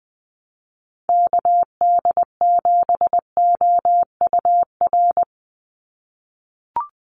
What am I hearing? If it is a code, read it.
Morse code, heard as a keyed tone, about 20 wpm, 700 Hz. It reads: KD7OUR
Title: USA Call Signs 20wpm